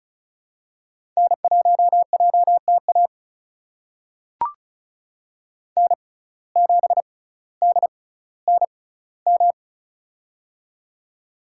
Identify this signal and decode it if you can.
Morse code, heard as a keyed tone, about 34 wpm, spaced as for 18 wpm, 700 Hz. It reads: N1JTA N7DNM
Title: USA Call Signs - 5 Characters - 6x Character Spacing - 35wpm